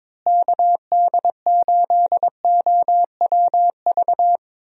Morse code, heard as a keyed tone, about 22 wpm, 700 Hz. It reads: KD8OWV
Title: USA Call Signs - 6 Characters - Repeated 2x - 22wpm